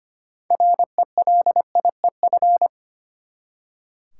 Morse code, heard as a keyed tone, about 25 wpm, 700 Hz. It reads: RELIEF
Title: Six Letter Words - 25wpm